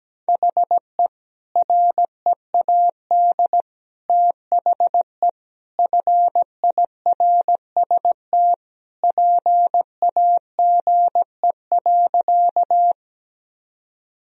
Morse code, heard as a keyed tone, about 17 wpm, 700 Hz. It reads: HE READ THE FIRST PAGE.
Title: Sentences from Top 1000 Words - Review Part 1 - 17wpm